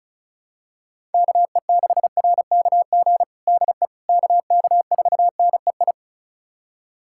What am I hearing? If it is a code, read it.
Morse code, heard as a keyed tone, about 36 wpm, 700 Hz. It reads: KE6RKG DE KK4NEI
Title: USA Callsign DE Callsign 35wpm